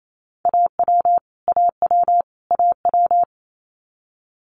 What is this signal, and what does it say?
Morse code, heard as a keyed tone, about 28 wpm, 700 Hz. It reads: AW AW AW
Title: Multi-letter Phonemes Repeated 3x 28wpm